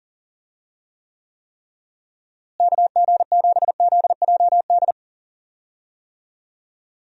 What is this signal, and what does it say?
Morse code, heard as a keyed tone, about 40 wpm, 700 Hz. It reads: KG7ZJD
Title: USA Call Signs - 6 Characters - Repeated 3x - 40wpm